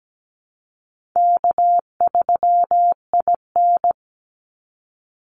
Morse code, heard as a keyed tone, about 17 wpm, 700 Hz. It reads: K3IN
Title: Easy Call Signs Repeated 2x - 17wpm